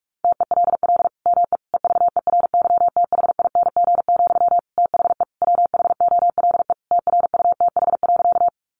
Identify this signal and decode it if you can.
Morse code, heard as a keyed tone, about 44 wpm, 700 Hz. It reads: TELL ME EVERYTHING, THE WHOLE TRUTH.
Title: Sentences from Top 800 Words 45wpm